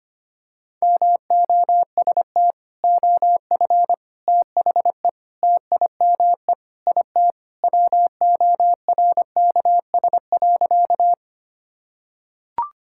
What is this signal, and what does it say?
Morse code, heard as a keyed tone, about 25 wpm, 700 Hz. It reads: MOST OF THE TIME IT WORKS.
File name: practice / 25wpm / Sentences with 3-Word Phrases - 25wpm